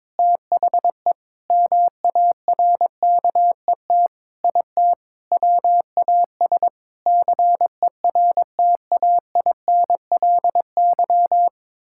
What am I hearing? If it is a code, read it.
Morse code, heard as a keyed tone, about 22 wpm, 700 Hz. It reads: THE MARKET IT WAS CERTAINLY
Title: Sentences from Top 1600 Words 22wpm